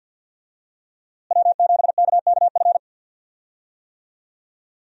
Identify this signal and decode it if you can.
Morse code, heard as a keyed tone, about 51 wpm, 700 Hz. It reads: W7KKF
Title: USA Call Signs - 5 Characters - Repeated 2x - 50wpm